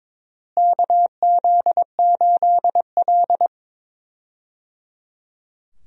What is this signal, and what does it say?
Morse code, heard as a keyed tone, about 22 wpm, 700 Hz. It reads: KZ8L
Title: USA Call Signs - 4 Characters - Repeated 3x - 22wpm